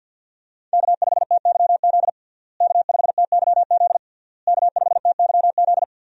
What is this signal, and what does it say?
Morse code, heard as a keyed tone, about 52 wpm, 700 Hz. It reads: K5TYZ K5TYZ K5TYZ
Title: USA Call Signs - 5 Characters - Repeated 3x - 50wpm